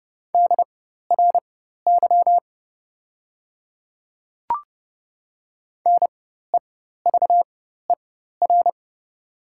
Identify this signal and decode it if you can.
Morse code, heard as a keyed tone, about 30 wpm, spaced as for 14 wpm, 700 Hz. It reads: DRY NEVER
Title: 4x Character Spacing - Top 500 Words - 30wpm